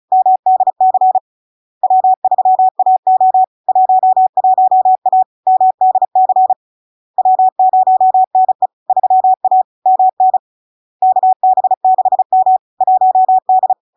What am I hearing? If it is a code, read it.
Morse code, heard as a keyed tone, about 36 wpm, 750 Hz. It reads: MDC W3AO 11A MDC W0NE 3A MN KB6M 1D
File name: LARK-FD-2022-Log-in-CW-35wpm-750-Hz